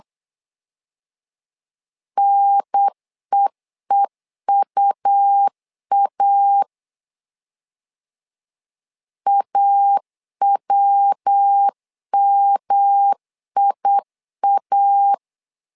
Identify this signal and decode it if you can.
Morse code, heard as a keyed tone, about 8 wpm, 800 Hz. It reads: NEEUA AWMIA
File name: LEZIONI / 3 MWRO / 3 MW 8wpm 2501